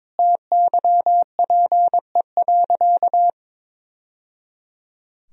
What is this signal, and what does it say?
Morse code, heard as a keyed tone, about 22 wpm, 700 Hz. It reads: TYPE.
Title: Sentences from Top 1000 Words - Review Part 3 - 22wpm